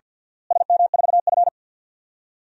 Morse code, heard as a keyed tone, about 50 wpm, 700 Hz. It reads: SM4F
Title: DXCC-100 Call Signs - 4 Characters - Repeated 2x - 50wpm